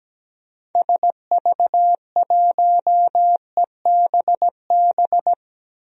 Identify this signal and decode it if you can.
Morse code, heard as a keyed tone, about 17 wpm, 700 Hz. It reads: SV1EBB
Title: DXCC-100 Call Signs - 6 Characters - 17wpm